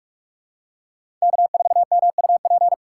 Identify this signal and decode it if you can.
Morse code, heard as a keyed tone, about 41 wpm, 700 Hz. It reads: K4MUP
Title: USA Call Signs - 5 Characters - 4x Character Spacing - 45wpm